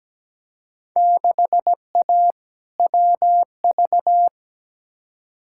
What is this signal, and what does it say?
Morse code, heard as a keyed tone, about 17 wpm, 700 Hz. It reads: 6A WV
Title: Field Day Exchanges - 17wpm